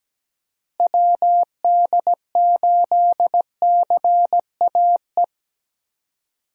Morse code, heard as a keyed tone, about 17 wpm, 700 Hz. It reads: WD8CAE
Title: USA Call Signs - 6 Characters - Repeated 3x - 17wpm